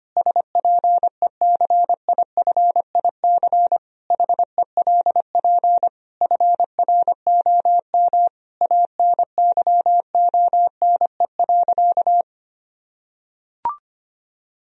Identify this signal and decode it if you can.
Morse code, heard as a keyed tone, about 25 wpm, 700 Hz. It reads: SPECIFIC HELP FROM ANYONE.
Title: Sentences from Top 1000 Words - Review Part 3 - 25wpm